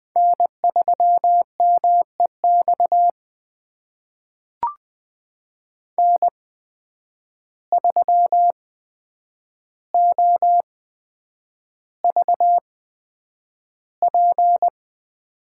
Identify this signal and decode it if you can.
Morse code, heard as a keyed tone, about 20 wpm, spaced as for 9 wpm, 700 Hz. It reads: N3MEX N3OVP
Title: USA Call Signs - 5 Characters - 8x Character Spacing - 20wpm